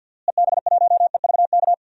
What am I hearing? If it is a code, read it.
Morse code, heard as a keyed tone, about 49 wpm, 700 Hz. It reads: EB1EVK
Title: DXCC-100 Call Signs - 6 Characters - Repeated 2x - 50wpm